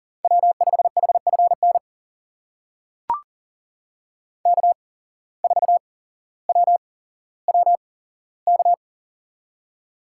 Morse code, heard as a keyed tone, about 39 wpm, spaced as for 17 wpm, 700 Hz. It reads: W5HFN K4WWK
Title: USA Call Signs - 5 Characters - 8x Character Spacing - 40wpm